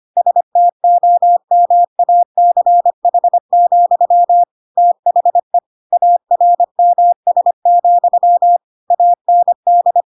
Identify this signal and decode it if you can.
Morse code, heard as a keyed tone, about 25 wpm, 700 Hz. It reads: STOMACH, THE ARMS, AND
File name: Pinocchio / Pinocchio ch3 25 25 ik1hge